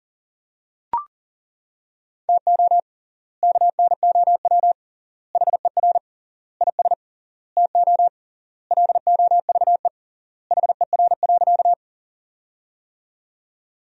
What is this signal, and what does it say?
Morse code, heard as a keyed tone, about 40 wpm, 700 Hz. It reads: TO KNOW HER IS TO LOVE HER.
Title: 3x Word-Spacing - Sentences from Top 400 Words - 40wpm